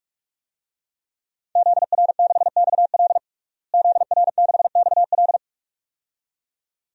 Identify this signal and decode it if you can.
Morse code, heard as a keyed tone, about 46 wpm, 700 Hz. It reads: ZR6XL ZR6XL
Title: DXCC-100 Call Signs - 5 Characters - Repeated 2x - 45wpm